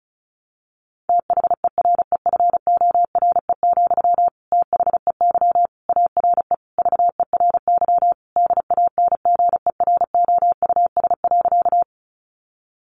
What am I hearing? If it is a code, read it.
Morse code, heard as a keyed tone, about 34 wpm, 700 Hz. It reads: THEREFORE, THEY ARE VERY DANGEROUS.
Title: Sentences from Top 1200 Words 35wpm